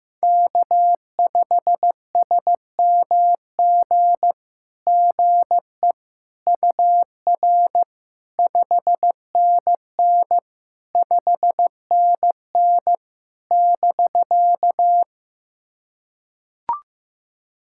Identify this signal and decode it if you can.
Morse code, heard as a keyed tone, about 15 wpm, 700 Hz. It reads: K5SMG GE UR 5NN 5NN <BK>
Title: SOTA QSO Practice 15wpm